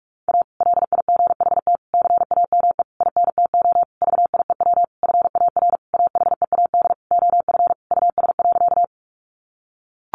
Morse code, heard as a keyed tone, about 44 wpm, 700 Hz. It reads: A LIGHT CAME INTO VIEW FAR AHEAD OF US.
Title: Sentences from Top 1100 Words 45wpm